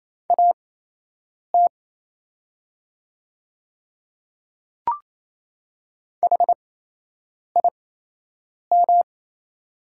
Morse code, heard as a keyed tone, about 28 wpm, spaced as for 8 wpm, 690 Hz. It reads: AT HIM
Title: 8x Character Spacing - Top 100 Words - 28wpm